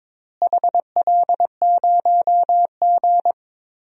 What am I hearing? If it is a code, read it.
Morse code, heard as a keyed tone, about 22 wpm, 700 Hz. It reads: HL0G
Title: DXCC-100 Call Signs - 4 Characters - Repeated 3x - 22wpm